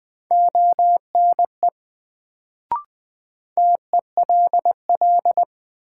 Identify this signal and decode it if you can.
Morse code, heard as a keyed tone, about 20 wpm, 700 Hz. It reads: ONE TELL
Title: Sets of 2 Words - New Word Drill - 20wpm